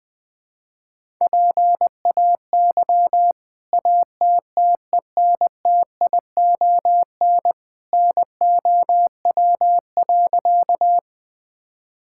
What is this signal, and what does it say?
Morse code, heard as a keyed tone, about 20 wpm, 700 Hz. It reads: PAY ATTENTION NOW.
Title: Sentences from Top 1000 Words - Review Part 6 - 20wpm